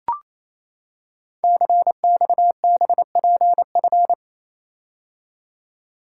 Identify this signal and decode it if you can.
Morse code, heard as a keyed tone, about 27 wpm, 700 Hz. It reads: CXBPF
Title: Code Groups - 5 Letters - 28wpm